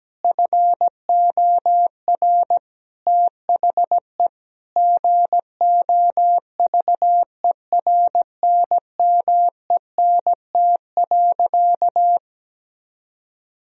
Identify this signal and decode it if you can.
Morse code, heard as a keyed tone, about 17 wpm, 700 Hz. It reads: FOR THE GOVERNMENT.
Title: Sentences from Top 1000 Words - Review Part 3 - 17wpm